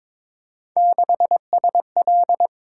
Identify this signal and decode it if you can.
Morse code, heard as a keyed tone, about 22 wpm, 700 Hz. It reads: 6SL
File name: practice / 22wpm / Three Letter-Number Combinations - 22wpm